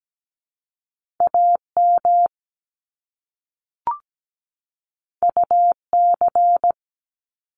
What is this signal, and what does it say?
Morse code, heard as a keyed tone, about 17 wpm, 700 Hz. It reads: AM UC
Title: Top Two Letter Combinations in Words 17wpm